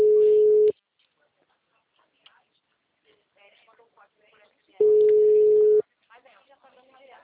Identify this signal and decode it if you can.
Ringback tone (Europe, 425 Hz), with speech.
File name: out- -2002- - - .79576